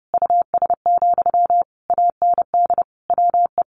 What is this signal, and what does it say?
Morse code, heard as a keyed tone, about 30 wpm, 700 Hz. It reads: US, AND WE
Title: Sentences from Top 1400 Words 30wpm